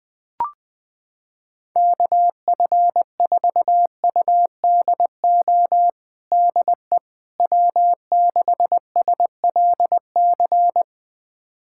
Morse code, heard as a keyed tone, about 21 wpm, 700 Hz. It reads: KF4UDO DE W6SLC
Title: USA Callsign DE Callsign 20wpm